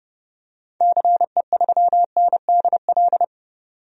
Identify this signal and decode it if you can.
Morse code, heard as a keyed tone, about 30 wpm, 700 Hz. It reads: CE3NDL
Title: DXCC-100 Call Signs - 6 Characters - 30wpm